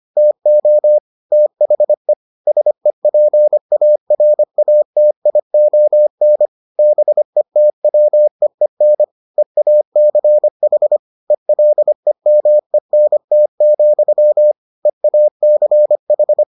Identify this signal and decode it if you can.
Morse code, heard as a keyed tone, about 25 wpm, 600 Hz. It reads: TO THE SEPARATION BETWEEN EACH ELEMENT, EACH